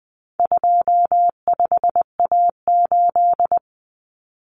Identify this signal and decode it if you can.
Morse code, heard as a keyed tone, about 20 wpm, 700 Hz. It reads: 25A8
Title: Common Cut Numbers - 4-Digit 20wpm